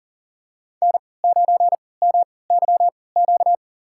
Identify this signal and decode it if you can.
Morse code, heard as a keyed tone, about 40 wpm, spaced as for 23 wpm, 700 Hz. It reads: N9MYQ
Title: USA Call Signs - 5 Characters - 3x Character Spacing - 40wpm